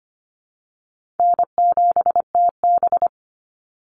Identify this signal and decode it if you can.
Morse code, heard as a keyed tone, about 25 wpm, 700 Hz. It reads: N7TB
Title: Call Signs Repeated 2x - 25wpm